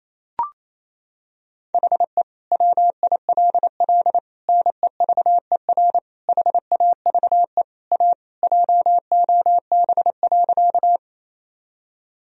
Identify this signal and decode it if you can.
Morse code, heard as a keyed tone, about 28 wpm, 700 Hz. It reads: HE WILL NEVER HAVE A JOB.
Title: Sentences from Top 1000 Words - Review Part 3 - 28wpm